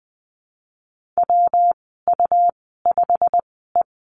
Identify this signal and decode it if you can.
Morse code, heard as a keyed tone, about 20 wpm, spaced as for 15 wpm, 700 Hz. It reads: WU5E
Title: 122 Lesson 20g - Number 3 - Call Signs 20wpm